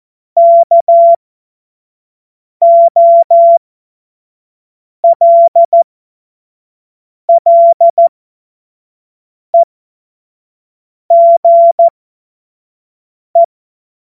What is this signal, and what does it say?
Morse code, heard as a keyed tone, about 14 wpm, spaced as for 5 wpm, 680 Hz. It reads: KOLLEGE
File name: Morsekurs / Witze / Witz 27 Februar 2024 25 F70